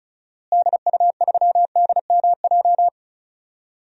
Morse code, heard as a keyed tone, about 36 wpm, 700 Hz. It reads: DU3DMJ
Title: DXCC-100 Call Signs - 6 Characters - Repeated 2x - 35wpm